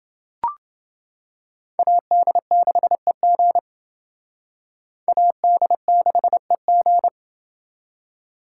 Morse code, heard as a keyed tone, about 28 wpm, 700 Hz. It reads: AD6EG AD6EG